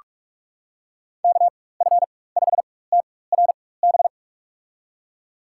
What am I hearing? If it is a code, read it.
Morse code, heard as a keyed tone, about 46 wpm, spaced as for 21 wpm, 700 Hz. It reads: KF5TRB
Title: USA Call Signs - 6 Characters - 4x Character Spacing - 45wpm